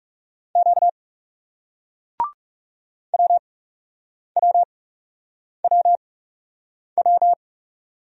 Morse code, heard as a keyed tone, about 37 wpm, spaced as for 8 wpm, 700 Hz. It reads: Q WWWW
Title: Speed-Racing Single Letters 30wpm